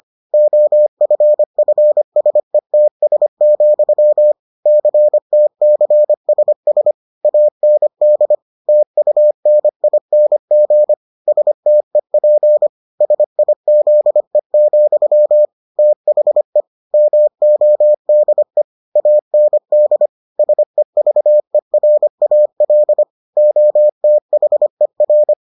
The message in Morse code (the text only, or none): OFFSETS, CTCSS AND TUNING STEP SIZE, THE MODE AND SEVERAL OTHER